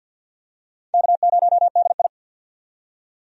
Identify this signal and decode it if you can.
Morse code, heard as a keyed tone, about 50 wpm, 700 Hz. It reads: K0DI
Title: USA Call Signs - 4 Characters - 50wpm